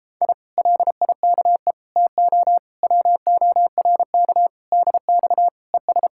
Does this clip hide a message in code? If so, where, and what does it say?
Yes: Morse code from 0.2 to 6.1 s, heard as a keyed tone, about 31 wpm, 700 Hz. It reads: I LIKE TO WORK DX ES